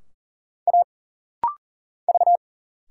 Morse code, heard as a keyed tone, about 40 wpm, 700 Hz. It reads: A V
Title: Single Letters - Rapid-Fire 40wpm